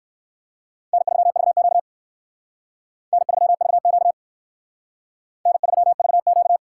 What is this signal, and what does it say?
Morse code, heard as a keyed tone, about 56 wpm, 700 Hz. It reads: N3VQ N3VQ N3VQ
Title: Speed Racing - USA 1x2 Call Signs 45wpm